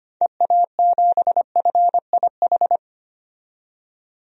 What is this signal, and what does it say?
Morse code, heard as a keyed tone, about 25 wpm, 700 Hz. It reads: EA7FIH
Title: DXCC-100 Call Signs - 6 Characters - 25wpm